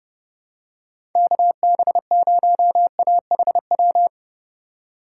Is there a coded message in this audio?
Yes: Morse code from 1.2 to 4.1 s, heard as a keyed tone, about 29 wpm, 700 Hz. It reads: KB0AHW